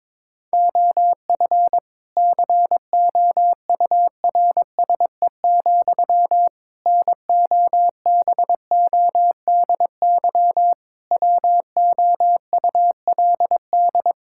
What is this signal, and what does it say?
Morse code, heard as a keyed tone, about 22 wpm, 700 Hz. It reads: OF COURSE, NOBODY WOULD
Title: Sentences from Top 1600 Words 22wpm